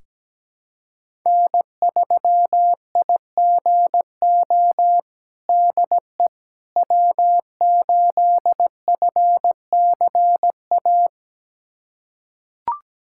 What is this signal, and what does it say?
Morse code, heard as a keyed tone, about 17 wpm, 700 Hz. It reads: N3IGO DE W8FCA
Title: USA Callsign DE Callsign - 5 Characters - 17wpm